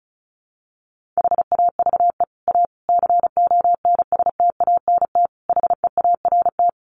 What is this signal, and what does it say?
Morse code, heard as a keyed tone, about 34 wpm, 700 Hz. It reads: HAVE A CONSTANT HEART
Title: Sentences from Top 1500 Words 35wpm